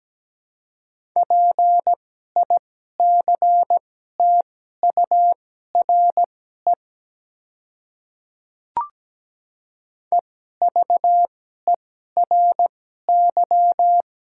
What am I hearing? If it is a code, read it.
Morse code, heard as a keyed tone, about 17 wpm, spaced as for 12 wpm, 700 Hz. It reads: PICTURE EVERY
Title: 2x Character Spacing - Top 300 Words - 17wpm